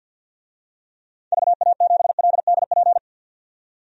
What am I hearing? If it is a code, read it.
Morse code, heard as a keyed tone, about 50 wpm, 700 Hz. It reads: VA7LDP
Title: DXCC-100 Call Signs - 6 Characters - 50wpm